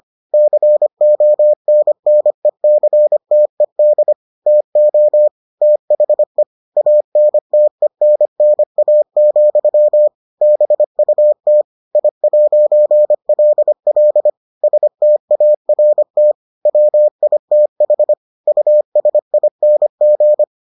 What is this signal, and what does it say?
Morse code, heard as a keyed tone, about 25 wpm, 600 Hz. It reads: CONNECTED TO THE ANTENNA, BUT I'LL START WITH USING